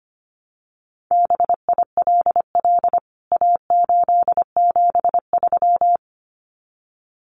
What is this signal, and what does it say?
Morse code, heard as a keyed tone, about 25 wpm, 700 Hz. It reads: BILL A873
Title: CWT Exchange 25wpm